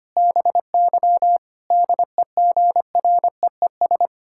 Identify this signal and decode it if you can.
Morse code, heard as a keyed tone, about 25 wpm, 700 Hz. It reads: BY DEGREES
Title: Pangram Challenge v2 - 25wpm